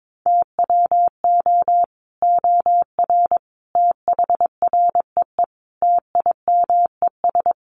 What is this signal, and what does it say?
Morse code, heard as a keyed tone, about 22 wpm, 700 Hz. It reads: TWO OR THREE TIMES
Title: Sets of 5 Words Encore2 22wpm